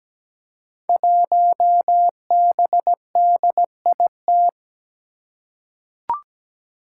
Morse code, heard as a keyed tone, about 17 wpm, 700 Hz. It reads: 1BDIT
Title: Five Challenging Letter-Number Combinations 17wpm